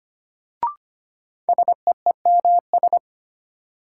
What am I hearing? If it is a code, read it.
Morse code, heard as a keyed tone, about 25 wpm, 700 Hz. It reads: SEEMS
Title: Sets of 3 Words - New Word Drill - 25wpm